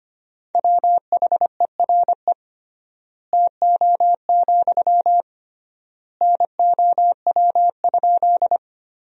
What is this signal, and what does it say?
Morse code, heard as a keyed tone, about 25 wpm, 700 Hz. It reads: WHERE TO, NOW?
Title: 3x Word-Spacing - Sentences from Top 200 Words - 25wpm